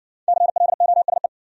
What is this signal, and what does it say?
Morse code, heard as a keyed tone, about 53 wpm, 700 Hz. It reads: CLOSE